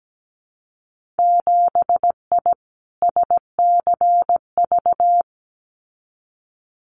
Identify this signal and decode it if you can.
Morse code, heard as a keyed tone, about 17 wpm, 700 Hz. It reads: 7I SCV
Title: Winter Field Day Exchanges Encore 17wpm